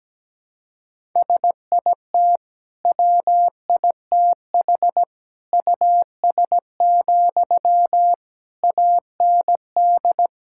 Morse code, heard as a keyed tone, about 17 wpm, 700 Hz. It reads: SIT WITH US, AND
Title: Sentences from Top 1600 Words 17wpm